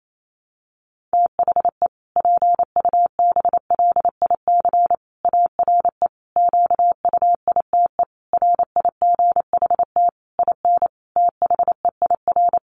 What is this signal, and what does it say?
Morse code, heard as a keyed tone, about 28 wpm, 700 Hz. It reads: THE PUBLIC ARE QUITE RIGHT IN THEIR
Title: Sentences from Top 1300 Words 28wpm